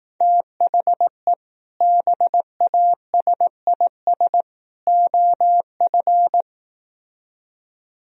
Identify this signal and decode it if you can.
Morse code, heard as a keyed tone, about 18 wpm, 700 Hz. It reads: THE BASIS OF
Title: Speed-Racing - Sets of 3 Words 15wpm